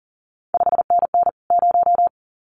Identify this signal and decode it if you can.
Morse code, heard as a keyed tone, about 40 wpm, 700 Hz. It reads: 5NN 0
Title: CQ WW DX Contest Exchange Encore - 40wpm